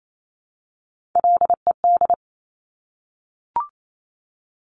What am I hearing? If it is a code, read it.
Morse code, heard as a keyed tone, about 28 wpm, 700 Hz. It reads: LED
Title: Top Three Letter Combinations in Words 28wpm